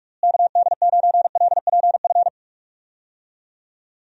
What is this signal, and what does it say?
Morse code, heard as a keyed tone, about 42 wpm, 700 Hz. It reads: KD9LPF
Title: USA Call Signs - 6 Characters - Repeated 2x - 45wpm